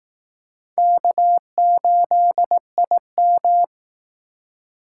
Morse code, heard as a keyed tone, about 19 wpm, 700 Hz. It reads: K8IM
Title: Speed Racing - USA 1x2 Call Signs 15wpm